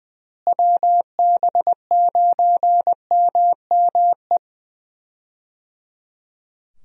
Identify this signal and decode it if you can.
Morse code, heard as a keyed tone, about 21 wpm, 700 Hz. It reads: WB9MME
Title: USA Call Signs - 6 Characters - Repeated 3x - 20wpm